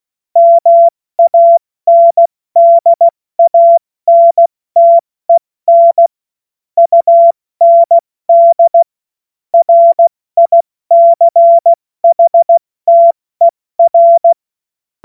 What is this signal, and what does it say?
Morse code, heard as a keyed tone, about 16 wpm, 680 Hz. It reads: MANDANTEN UND RICHTER
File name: Morsekurs / Witze / Witz 09. August 2022 70 F80